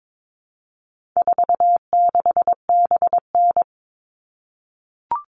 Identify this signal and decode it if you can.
Morse code, heard as a keyed tone, about 22 wpm, 700 Hz. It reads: 46BN
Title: All Cut Numbers - 4-Digit 22wpm